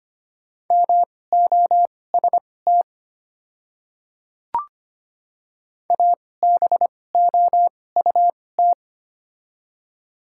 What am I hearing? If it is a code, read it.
Morse code, heard as a keyed tone, about 25 wpm, spaced as for 18 wpm, 700 Hz. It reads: MOST ABOUT
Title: 2x Character Spacing - Top 100 Words - 25wpm